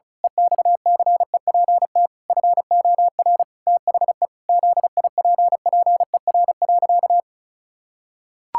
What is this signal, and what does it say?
Morse code, heard as a keyed tone, about 36 wpm, 700 Hz. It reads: EXCEPT FOR THE ZIPPER.
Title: Pangram Challenge v2 - 35wpm